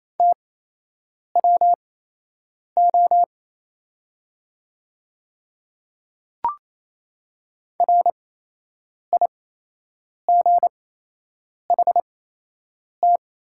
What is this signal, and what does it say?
Morse code, heard as a keyed tone, about 27 wpm, spaced as for 8 wpm, 700 Hz. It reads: TWO RIGHT
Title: 8x Character Spacing - Top 300 Words - 28wpm